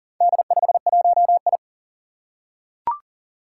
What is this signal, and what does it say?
Morse code, heard as a keyed tone, about 37 wpm, 700 Hz. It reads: D51I
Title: Four Challenging Letter-Number Combinations 40wpm